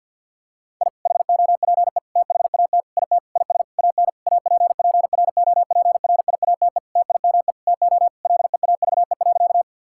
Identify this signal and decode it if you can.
Morse code, heard as a keyed tone, about 50 wpm, 700 Hz. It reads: I HOPE THAT IT IS AN APPROPRIATE TIME TO LEAVE.
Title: Sentences from Top 1000 Words - Review Part 5 - 50wpm